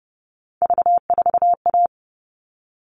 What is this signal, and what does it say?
Morse code, heard as a keyed tone, about 30 wpm, 700 Hz. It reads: V4A
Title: All Cut Numbers - 3-Digit 30wpm